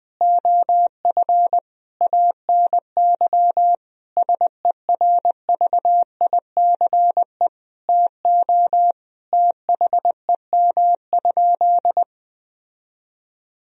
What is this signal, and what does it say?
Morse code, heard as a keyed tone, about 20 wpm, 700 Hz. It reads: OF ANY SERVICE TO THEM?
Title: Sentences from Top 1000 Words - Review Part 4 - 20wpm